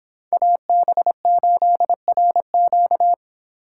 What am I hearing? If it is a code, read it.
Morse code, heard as a keyed tone, about 31 wpm, 700 Hz. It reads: AB8RQ